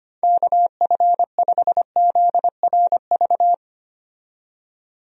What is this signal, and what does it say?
Morse code, heard as a keyed tone, about 25 wpm, 700 Hz. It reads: KF5ZRV
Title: USA Call Signs - 6 Characters - 25wpm